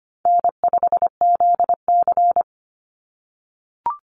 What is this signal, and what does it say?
Morse code, heard as a keyed tone, about 25 wpm, 700 Hz. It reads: N5ZC
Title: NA SOTA Chaser Callsigns 25wpm